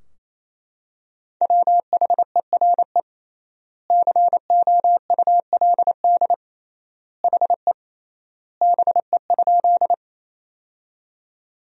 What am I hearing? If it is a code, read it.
Morse code, heard as a keyed tone, about 28 wpm, 700 Hz. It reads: WHERE COULD HE BE?
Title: 3x Word-Spacing - Sentences from Top 200 Words - 28wpm